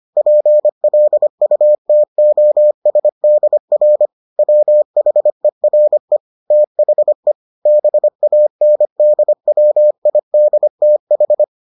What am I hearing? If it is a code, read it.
Morse code, heard as a keyed tone, about 25 wpm, 600 Hz. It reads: PLUTOSDR WHERE THE BANDWIDTH